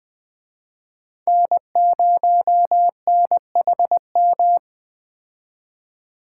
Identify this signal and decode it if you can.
Morse code, heard as a keyed tone, about 20 wpm, 700 Hz. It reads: N0NHM